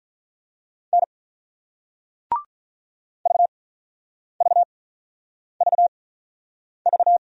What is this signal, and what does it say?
Morse code, heard as a keyed tone, about 48 wpm, 700 Hz. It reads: N V V V V